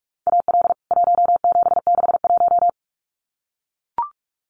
Morse code, heard as a keyed tone, about 44 wpm, 700 Hz. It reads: AL 1761
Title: CWT Exchange 45wpm